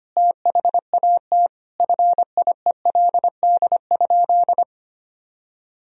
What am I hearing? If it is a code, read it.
Morse code, heard as a keyed tone, about 25 wpm, 700 Hz. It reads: THAT FIELD?
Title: Sentences from Top 1000 Words - Review Part 2 - 25wpm